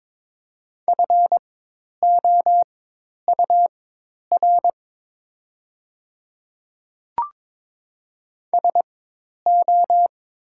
Morse code, heard as a keyed tone, about 23 wpm, spaced as for 10 wpm, 700 Hz. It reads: FOUR SO
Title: 4x Character Spacing - Top 500 Words - 22wpm